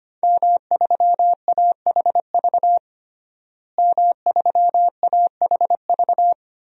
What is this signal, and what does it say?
Morse code, heard as a keyed tone, about 25 wpm, 700 Hz. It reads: M3AHV M3AHV
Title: DXCC-100 Call Signs - 5 Characters - Repeated 2x - 25wpm